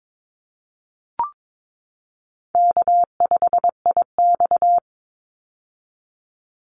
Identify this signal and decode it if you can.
Morse code, heard as a keyed tone, about 22 wpm, 700 Hz. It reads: K5IX
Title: Call Signs - 22wpm